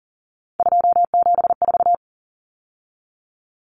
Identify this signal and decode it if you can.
Morse code, heard as a keyed tone, about 40 wpm, 700 Hz. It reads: 274
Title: Numbers - 3-Digit 40wpm